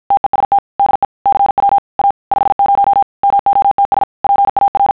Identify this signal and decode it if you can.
Morse code, heard as a keyed tone, about 50 wpm, 800 Hz. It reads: TEST DE CW A 50 MOTS PAR
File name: sons / TEST50